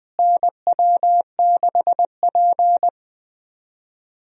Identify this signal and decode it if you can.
Morse code, heard as a keyed tone, about 20 wpm, 700 Hz. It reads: NW6P
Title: USA Call Signs - 4 Characters - Repeated 3x - 20wpm